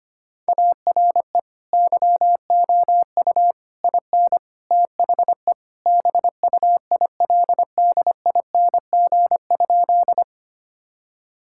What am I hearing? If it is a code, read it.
Morse code, heard as a keyed tone, about 25 wpm, 700 Hz. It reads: ARE YOU IN THE BUILDING?
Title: Sentences from Top 1000 Words - Review Part 5 - 25wpm